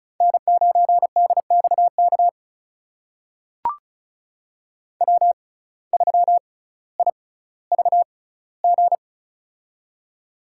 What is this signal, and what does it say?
Morse code, heard as a keyed tone, about 36 wpm, spaced as for 18 wpm, 700 Hz. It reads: N9DXK W3IVG
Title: USA Call Signs - 5 Characters - 6x Character Spacing - 35wpm